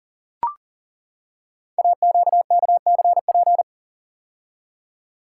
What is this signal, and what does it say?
Morse code, heard as a keyed tone, about 39 wpm, 700 Hz. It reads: AQKCP
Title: Code Groups - 5 Letters - 40wpm